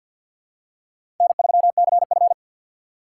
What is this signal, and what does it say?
Morse code, heard as a keyed tone, about 48 wpm, 700 Hz. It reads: N3CF
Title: USA Call Signs - 4 Characters - 3x Character Spacing - 50wpm